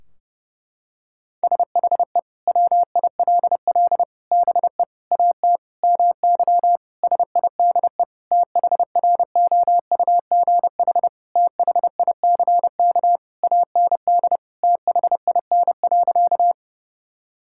Morse code, heard as a keyed tone, about 32 wpm, 700 Hz. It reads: SHE WILL BE AT MY SIDE THROUGH THICK AND THIN.
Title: Sentences from Top 1600 Words 30wpm